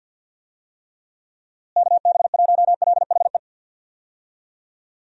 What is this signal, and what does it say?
Morse code, heard as a keyed tone, about 48 wpm, 700 Hz. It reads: KB1LHE
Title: USA Call Signs - 6 Characters - Repeated 2x - 50wpm